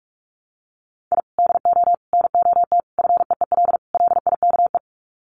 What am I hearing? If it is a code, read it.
Morse code, heard as a keyed tone, about 44 wpm, 700 Hz. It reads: I DO NOT FEEL LIKE
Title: Sets of 5 Words Encore 45wpm